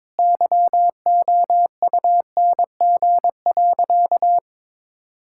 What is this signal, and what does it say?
Morse code, heard as a keyed tone, about 22 wpm, 700 Hz. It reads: YOUNG.
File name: practice / 22wpm / Sentences with 3-Word Phrases - 22wpm